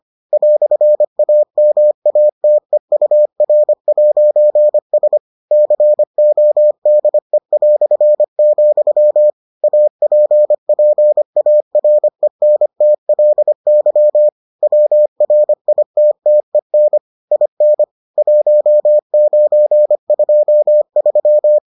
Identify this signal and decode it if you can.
Morse code, heard as a keyed tone, about 25 wpm, 600 Hz. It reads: "AMATEUR'S CODE", APPARENTLY WRITTEN IN 1923